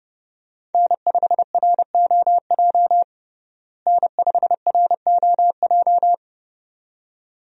Morse code, heard as a keyed tone, about 30 wpm, 700 Hz. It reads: N5ROJ N5ROJ
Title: USA Call Signs - 5 Characters - Repeated 2x - 30wpm